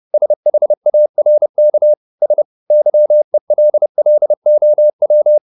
Morse code, heard as a keyed tone, about 30 wpm, 600 Hz. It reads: SHARK S YELLOW